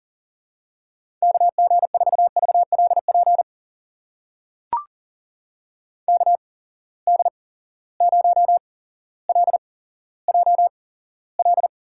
Morse code, heard as a keyed tone, about 38 wpm, spaced as for 17 wpm, 700 Hz. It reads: KG4VLP KD0LJL
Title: USA Call Signs - 6 Characters - 8x Character Spacing - 40wpm